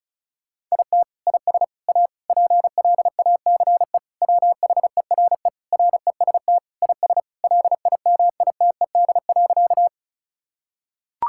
Morse code, heard as a keyed tone, about 36 wpm, 700 Hz. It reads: IT IS A PLACE WHERE REST IS LIMITED.
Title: Sentences from Top 1000 Words - Review Part 6 - 35wpm